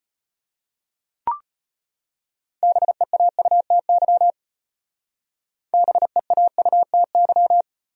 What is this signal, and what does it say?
Morse code, heard as a keyed tone, about 34 wpm, 700 Hz. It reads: BEAUTY BEAUTY